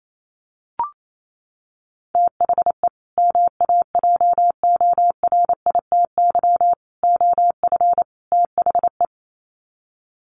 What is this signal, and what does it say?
Morse code, heard as a keyed tone, about 28 wpm, 700 Hz. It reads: THE MAJORITY OF THE
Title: Sets of 4 Words 28wpm